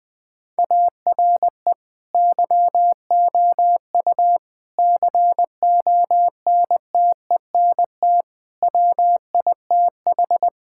Morse code, heard as a keyed tone, about 20 wpm, 700 Hz. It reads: ARE YOU CONTENT WITH
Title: Sentences from Top 1000 Words - Review Part 6 - 20wpm